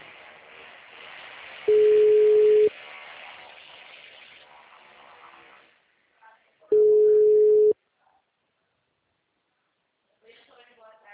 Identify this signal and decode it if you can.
Ringback tone (Europe, 425 Hz), with speech.